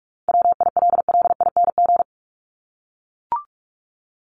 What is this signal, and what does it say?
Morse code, heard as a keyed tone, about 44 wpm, 700 Hz. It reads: WILLING
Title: ING Suffix - Top 100 Words with Base Words 45wpm